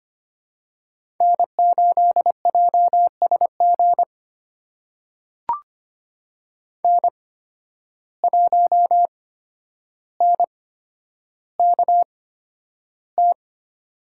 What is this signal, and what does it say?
Morse code, heard as a keyed tone, about 25 wpm, spaced as for 11 wpm, 700 Hz. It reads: N8JSG N1NKT
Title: USA Call Signs - 5 Characters - 8x Character Spacing - 25wpm